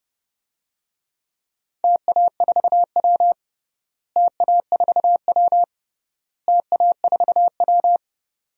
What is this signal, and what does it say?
Morse code, heard as a keyed tone, about 30 wpm, 700 Hz. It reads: TA4W TA4W TA4W
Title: DXCC-100 Call Signs - 4 Characters - Repeated 3x - 30wpm